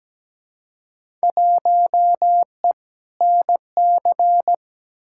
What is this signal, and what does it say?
Morse code, heard as a keyed tone, about 17 wpm, 700 Hz. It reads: 1E NC
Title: Field Day Exchanges - 17wpm